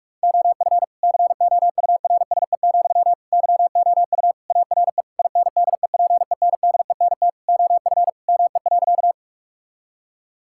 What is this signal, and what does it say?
Morse code, heard as a keyed tone, about 41 wpm, 700 Hz. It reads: OF COURSE, YOU ARE INDEPENDENT OF ME.
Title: Sentences from Top 1000 Words - Review Part 6 - 45wpm